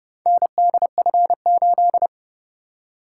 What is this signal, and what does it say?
Morse code, heard as a keyed tone, about 30 wpm, 700 Hz. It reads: NDF8